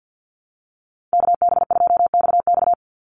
Morse code, heard as a keyed tone, about 49 wpm, 700 Hz. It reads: KB2XX
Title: NA SOTA Chaser Callsigns 50wpm